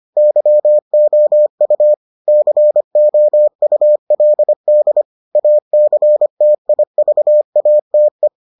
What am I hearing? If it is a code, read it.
Morse code, heard as a keyed tone, about 25 wpm, 600 Hz. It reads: YOU COULD ACTIVATE